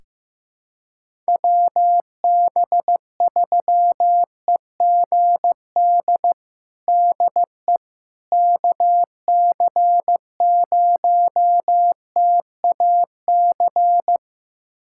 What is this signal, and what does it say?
Morse code, heard as a keyed tone, about 15 wpm, 700 Hz. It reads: WB3EGD DE KC0TAC
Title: USA Callsign DE Callsign - 6 Characters - 15wpm